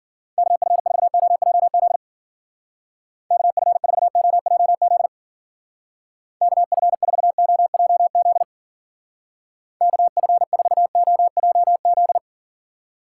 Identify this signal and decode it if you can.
Morse code, heard as a keyed tone, about 49 wpm, 700 Hz. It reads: KF4OJZ KF4OJZ KF4OJZ KF4OJZ
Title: Speed Racing - USA Call Signs 40wpm